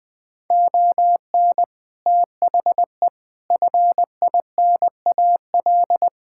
Morse code, heard as a keyed tone, about 20 wpm, 700 Hz. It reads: ON THE FINAL
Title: Sentences with 7-Word Phrases - 20wpm